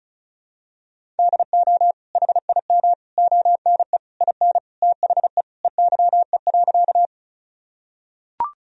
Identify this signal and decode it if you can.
Morse code, heard as a keyed tone, about 36 wpm, 700 Hz. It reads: DO HIM ONE IN THE EYE.
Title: Sentences from Top 1000 Words - Review Part 1 - 35wpm